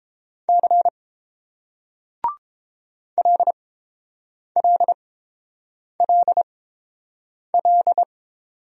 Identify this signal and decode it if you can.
Morse code, heard as a keyed tone, about 28 wpm, spaced as for 7 wpm, 690 Hz. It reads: C LLLL